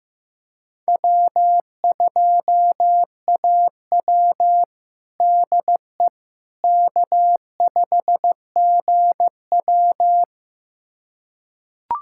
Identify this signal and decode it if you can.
Morse code, heard as a keyed tone, about 16 wpm, 700 Hz. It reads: W2AW DE K5GW
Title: USA Callsign DE Callsign - 4 Characters - 15wpm